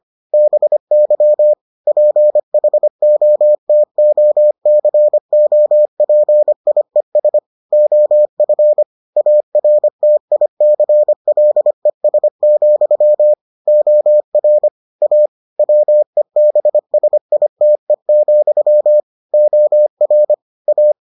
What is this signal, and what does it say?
Morse code, heard as a keyed tone, about 25 wpm, 600 Hz. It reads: BY PHOTOCOPIES OF ARTICLES, OR A WEBSITE, OR A